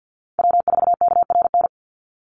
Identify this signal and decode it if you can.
Morse code, heard as a keyed tone, about 50 wpm, 700 Hz. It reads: W4KRN
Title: NA SOTA Chaser Callsigns 50wpm